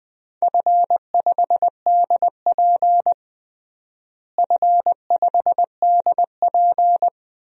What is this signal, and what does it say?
Morse code, heard as a keyed tone, about 20 wpm, 700 Hz. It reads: F5DP F5DP
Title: DXCC-100 Call Signs - 4 Characters - Repeated 3x - 20wpm